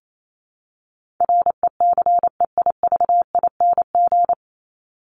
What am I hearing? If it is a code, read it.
Morse code, heard as a keyed tone, about 28 wpm, 700 Hz. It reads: RECEIVING
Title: ING Suffix - Top 100 Words with Base Words 28wpm